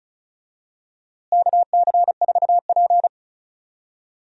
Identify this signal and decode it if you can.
Morse code, heard as a keyed tone, about 36 wpm, 700 Hz. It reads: KC4P
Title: USA Call Signs - 4 Characters - Repeated 2x - 35wpm